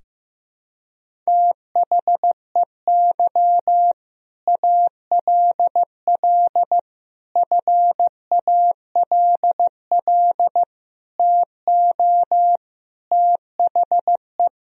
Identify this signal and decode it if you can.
Morse code, heard as a keyed tone, about 15 wpm, 700 Hz. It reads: THEY ALL FALL TO THE
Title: Sentences from Top 500 Words and Abbreviations 15wpm